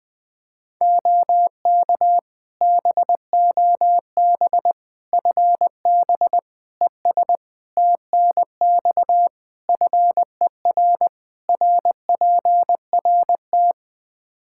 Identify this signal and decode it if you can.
Morse code, heard as a keyed tone, about 21 wpm, 700 Hz. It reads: OK BOB FB ES TNX FER RPRT
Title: Practice QSO - Exchange 2 - Info - 20wpm